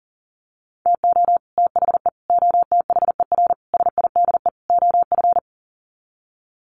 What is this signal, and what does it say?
Morse code, heard as a keyed tone, about 37 wpm, 700 Hz. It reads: TO THE OTHER SIDE OF
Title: Sets of 5 Words Encore 40wpm